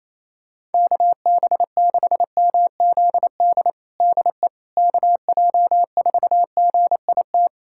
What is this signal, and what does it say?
Morse code, heard as a keyed tone, about 28 wpm, 700 Hz. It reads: KB6MZD DE KJ4GIT
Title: USA Callsign DE Callsign - 6 Characters - 28wpm